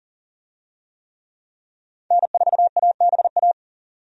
Morse code, heard as a keyed tone, about 40 wpm, 700 Hz. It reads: N4ABA
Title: USA Call Signs - 5 Characters - 40wpm